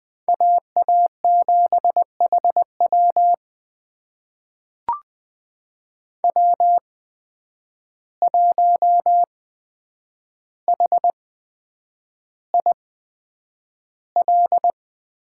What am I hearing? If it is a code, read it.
Morse code, heard as a keyed tone, about 20 wpm, spaced as for 9 wpm, 700 Hz. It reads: AA7HW W1HIL
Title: USA Call Signs - 5 Characters - 8x Character Spacing - 20wpm